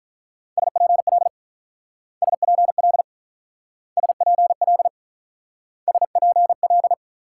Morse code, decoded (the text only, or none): SPL SPL SPL SPL